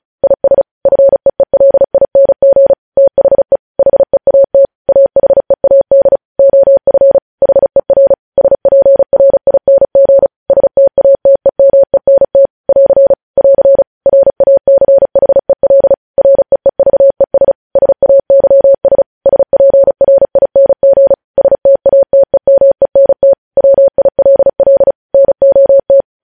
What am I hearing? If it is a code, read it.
Morse code, heard as a keyed tone, about 36 wpm, 550 Hz. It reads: IS FEELING THE HEAT AHEAD OF HER SPRING STATEMENT <AR> <AR> RACHEL REEVES SAYS SPRING STATEMENT WILL NOT
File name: podcast / News headlines-1x-35x35